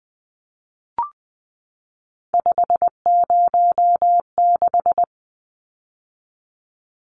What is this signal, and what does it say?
Morse code, heard as a keyed tone, about 20 wpm, 700 Hz. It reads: 506
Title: Numbers - 3-Digit 20wpm